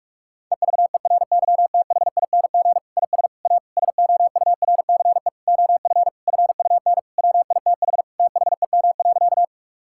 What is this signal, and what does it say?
Morse code, heard as a keyed tone, about 46 wpm, 700 Hz. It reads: EVERYTHING IS A SOURCE OF FUN WITH THEM.
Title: Sentences from Top 1000 Words - Review Part 5 - 45wpm